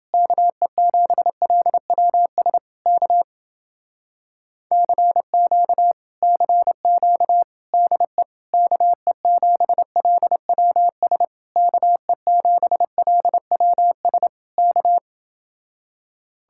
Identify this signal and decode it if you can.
Morse code, heard as a keyed tone, about 30 wpm, 700 Hz. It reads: KE7LWS K CQ CQ DE KE7LWS KE7LWS K